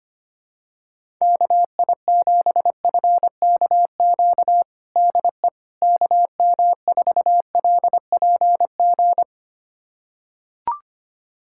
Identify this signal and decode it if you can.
Morse code, heard as a keyed tone, about 25 wpm, 700 Hz. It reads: KI7FKQ DE KM4LPG
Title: USA Callsign DE Callsign - 6 Characters - 25wpm